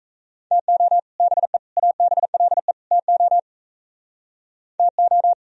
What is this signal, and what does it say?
Morse code, heard as a keyed tone, about 34 wpm, 700 Hz. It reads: TO BE ABLE TO TO
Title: Speed-Racing - Sets of 4 Words 28wpm